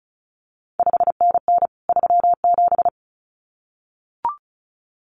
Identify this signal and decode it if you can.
Morse code, heard as a keyed tone, about 36 wpm, 700 Hz. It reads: 5NN 37
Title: CQ WW DX Contest Exchange Encore - 35wpm